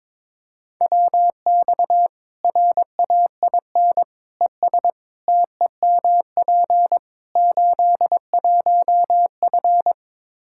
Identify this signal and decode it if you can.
Morse code, heard as a keyed tone, about 22 wpm, 700 Hz. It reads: WX RAIN ES TEMP 81F
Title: Practice QSO - Exchange 2 - Info - 22wpm